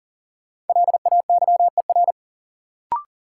Morse code, heard as a keyed tone, about 40 wpm, 700 Hz. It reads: LAYER
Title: Sentences from Top 1600 Words - New Word Drill - 40wpm